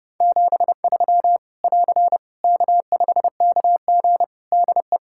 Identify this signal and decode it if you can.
Morse code, heard as a keyed tone, about 30 wpm, 700 Hz. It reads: 73 <AR> K5KG DE
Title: K1USN SST QSO Practice 30wpm